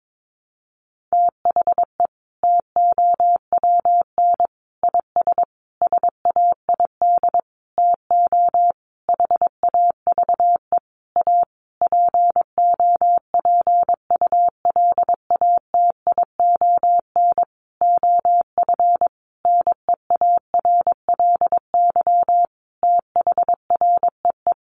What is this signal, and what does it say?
Morse code, heard as a keyed tone, about 22 wpm, 700 Hz. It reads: THE TOWN IS SAID TO HAVE A POPULATION OF NEARLY THREE
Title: Sentences from Top 1300 Words 22wpm